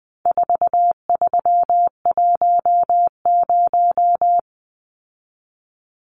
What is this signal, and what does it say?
Morse code, heard as a keyed tone, about 20 wpm, 700 Hz. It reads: 4310
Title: Numbers - 4-Digit 20wpm